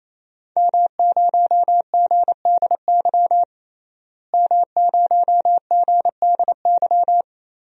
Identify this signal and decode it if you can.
Morse code, heard as a keyed tone, about 28 wpm, 700 Hz. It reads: M0GDY M0GDY
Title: DXCC-100 Call Signs - 5 Characters - Repeated 3x - 28wpm